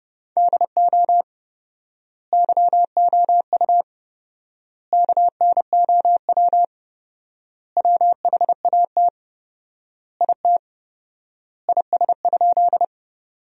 Morse code, heard as a keyed tone, about 30 wpm, 700 Hz. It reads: DO YOU KNOW WHAT IT IS?
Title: 4x Word-Spacing - Sentences from Top 100 Words - 30wpm